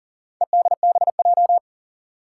Morse code, heard as a keyed tone, about 40 wpm, 700 Hz. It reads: EDBJ
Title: Four Challenging Letter-Number Combinations 40wpm